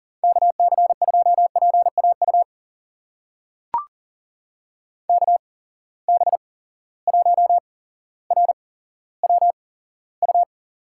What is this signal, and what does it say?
Morse code, heard as a keyed tone, about 40 wpm, spaced as for 17 wpm, 700 Hz. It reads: KC2PAU KB1RWU
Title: USA Call Signs - 6 Characters - 8x Character Spacing - 40wpm